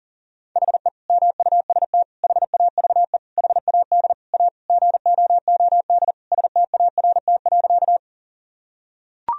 Morse code, heard as a keyed tone, about 40 wpm, 700 Hz. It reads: HE MUST HAVE HAD A GOOD START.
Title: Sentences from Top 1000 Words - Review Part 1 - 40wpm